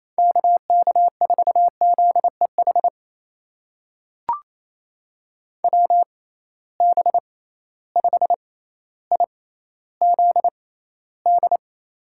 Morse code, heard as a keyed tone, about 28 wpm, spaced as for 14 wpm, 700 Hz. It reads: KK4ZEH WB5IZD
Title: USA Call Signs - 6 Characters - 6x Character Spacing - 28wpm